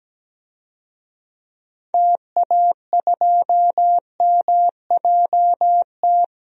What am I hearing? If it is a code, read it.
Morse code, heard as a keyed tone, about 17 wpm, 700 Hz. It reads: TA2MJT
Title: DXCC-100 Call Signs - 6 Characters - 17wpm